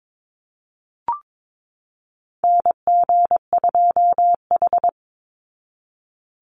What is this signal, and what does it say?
Morse code, heard as a keyed tone, about 22 wpm, 700 Hz. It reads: NG2H
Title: Call Signs Repeated 2x - 22wpm